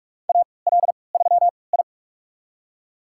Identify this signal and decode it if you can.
Morse code, heard as a keyed tone, about 45 wpm, spaced as for 25 wpm, 700 Hz. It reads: AL3I
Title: USA Call Signs - 4 Characters - 3x Character Spacing - 45wpm